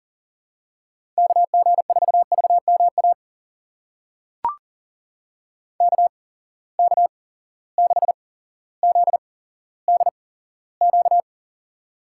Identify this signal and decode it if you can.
Morse code, heard as a keyed tone, about 40 wpm, spaced as for 17 wpm, 700 Hz. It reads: KG4VMA KK6ZDQ
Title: USA Call Signs - 6 Characters - 8x Character Spacing - 40wpm